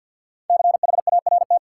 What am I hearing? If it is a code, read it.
Morse code, heard as a keyed tone, about 53 wpm, 700 Hz. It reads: CHART
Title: Sentences from Top 1300 Words - New Word Drill Alt - 50wpm